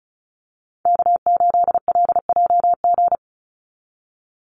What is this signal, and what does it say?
Morse code, heard as a keyed tone, about 36 wpm, 700 Hz. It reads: K8LJG
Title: NA SOTA Chaser Callsigns 35wpm